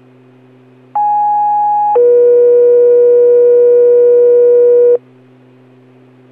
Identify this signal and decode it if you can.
Two-tone page, heard as a one-second tone, then a three-second tone, about 830 then 480 Hz. It is heard from 0.9 to 5.0 s.